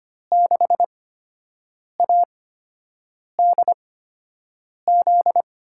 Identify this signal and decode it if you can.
Morse code, heard as a keyed tone, about 25 wpm, spaced as for 7 wpm, 700 Hz. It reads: 6ADZ
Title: USA Call Signs - 8x Character Spacing - 25wpm